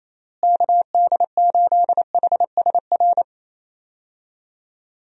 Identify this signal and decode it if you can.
Morse code, heard as a keyed tone, about 27 wpm, 700 Hz. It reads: KD8HSR
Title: USA Call Signs - 6 Characters - Repeated 2x - 28wpm